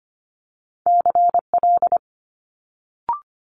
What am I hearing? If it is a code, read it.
Morse code, heard as a keyed tone, about 25 wpm, 700 Hz. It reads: CL
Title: Multi-letter Phonemes Quick Fire 25wpm